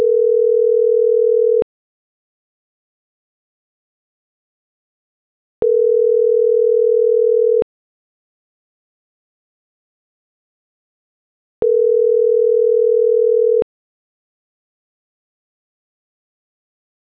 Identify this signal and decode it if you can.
Ringback tone (North America), 440 + 480 Hz.